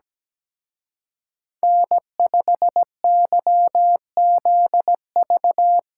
Morse code, heard as a keyed tone, about 20 wpm, 700 Hz. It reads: N5YZV
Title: Speed Racing - USA 1x3 Call Signs 17wpm